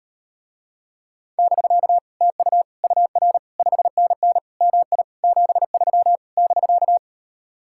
Morse code, heard as a keyed tone, about 39 wpm, 700 Hz. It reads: <BK> TU UR 5NN MI 73 <BK>